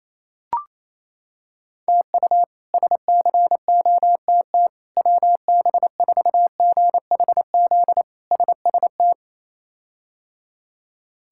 Morse code, heard as a keyed tone, about 27 wpm, 700 Hz. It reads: TU SCOTT WB4GHZ SST
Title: K1USN SST QSO Practice - 28wpm